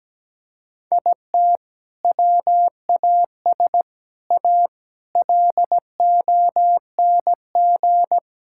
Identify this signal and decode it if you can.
Morse code, heard as a keyed tone, about 17 wpm, 700 Hz. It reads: IT WAS A LONG
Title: Sentences from Top 1600 Words 17wpm